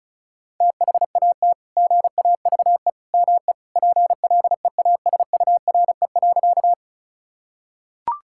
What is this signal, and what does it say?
Morse code, heard as a keyed tone, about 34 wpm, 700 Hz. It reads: THAT GAVE ME PLEASURE.
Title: Sentences from Top 1600 Words 35wpm